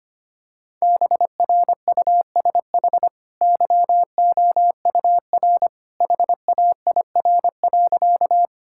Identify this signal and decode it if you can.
Morse code, heard as a keyed tone, about 25 wpm, 700 Hz. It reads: BRUSH YOUR HAIR.
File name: Easy Sentences with Hints - 25wpm